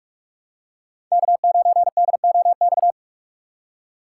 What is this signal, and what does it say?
Morse code, heard as a keyed tone, about 43 wpm, 700 Hz. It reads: K9DOX
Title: USA Call Signs - 5 Characters - 3x Character Spacing - 45wpm